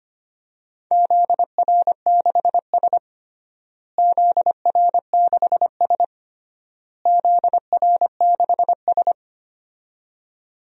Morse code, heard as a keyed tone, about 25 wpm, 700 Hz. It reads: ZR6S ZR6S ZR6S
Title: DXCC-100 Call Signs - 4 Characters - Repeated 3x - 25wpm